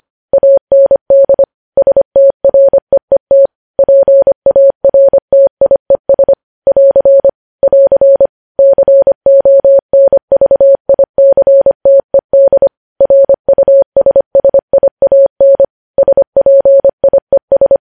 Morse code, heard as a keyed tone, about 25 wpm, 550 Hz. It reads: AND STREET PARTIES <AR> <AR> CONVICTED RUSSIAN SPIES
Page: news-headlines-in-morse-code-at-25-wpm